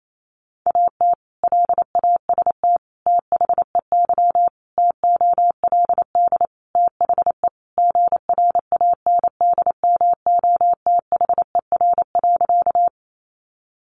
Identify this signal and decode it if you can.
Morse code, heard as a keyed tone, about 28 wpm, 700 Hz. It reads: AT LAST THEY TOLD THE GRANDMOTHER.
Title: Sentences from Top 1200 Words 28wpm